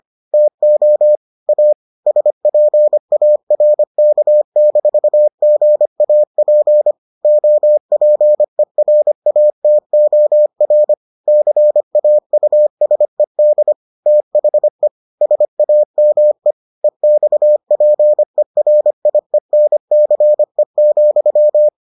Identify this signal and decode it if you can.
Morse code, heard as a keyed tone, about 25 wpm, 600 Hz. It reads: TO A SPARK-GAP OPERATOR CAUSED THE SAME EXPERIENCE,